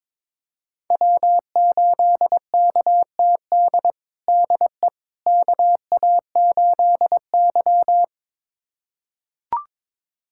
Morse code, heard as a keyed tone, about 21 wpm, 700 Hz. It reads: W8KTD DE KA8Y
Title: USA Callsign DE Callsign 22wpm